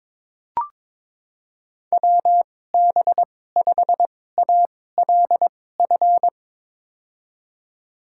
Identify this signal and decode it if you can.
Morse code, heard as a keyed tone, about 22 wpm, spaced as for 16 wpm, 700 Hz. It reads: WB5ALF
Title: USA Call Signs - 6 Characters - 2x Character Spacing - 22wpm